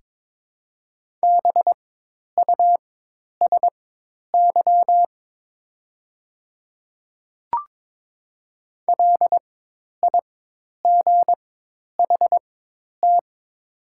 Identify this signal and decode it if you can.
Morse code, heard as a keyed tone, about 23 wpm, spaced as for 10 wpm, 700 Hz. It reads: BUSY LIGHT
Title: 4x Character Spacing - Top 500 Words - 22wpm